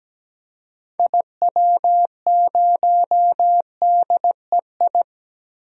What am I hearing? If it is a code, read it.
Morse code, heard as a keyed tone, about 17 wpm, 700 Hz. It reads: IW0DEI
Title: DXCC-100 Call Signs - 6 Characters - 17wpm